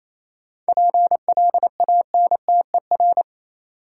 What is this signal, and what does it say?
Morse code, heard as a keyed tone, about 27 wpm, 700 Hz. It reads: PLANTER
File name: Easy Words - Speed-Racing with Hints - 22wpm